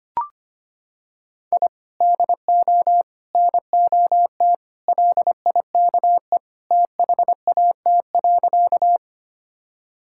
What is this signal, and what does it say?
Morse code, heard as a keyed tone, about 25 wpm, 700 Hz. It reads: I DO NOT LIKE THAT.
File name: practice / 25wpm / Sentences with 4-Word Phrases - 25wpm